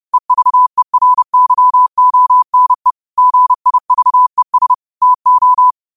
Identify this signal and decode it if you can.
Morse code, heard as a keyed tone, about 30 wpm, 1000 Hz. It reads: EVERYONE GIVES TO
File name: qotdfiles / 30wpm